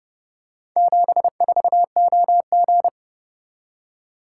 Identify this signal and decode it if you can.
Morse code, heard as a keyed tone, about 29 wpm, 700 Hz. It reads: 74OG